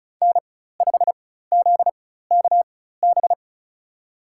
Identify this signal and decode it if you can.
Morse code, heard as a keyed tone, about 34 wpm, spaced as for 16 wpm, 700 Hz. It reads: N5ZKB
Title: USA Call Signs - 5 Characters - 4x Character Spacing - 35wpm